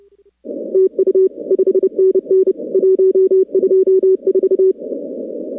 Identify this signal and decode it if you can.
Morse code, heard as a keyed tone, about 29 wpm, 380 Hz. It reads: TU 5NN 124
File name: contests / 2019-cw-wpx-kt5j-k5pi- -2330 0-41